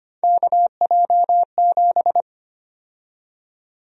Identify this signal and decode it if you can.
Morse code, heard as a keyed tone, about 25 wpm, 700 Hz. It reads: KJ7
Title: USA Call Sign Prefixes - Repeated 2x - 25wpm